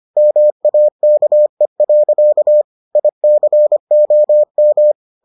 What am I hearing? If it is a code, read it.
Morse code, heard as a keyed tone, about 25 wpm, 600 Hz. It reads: MAKE. ICOM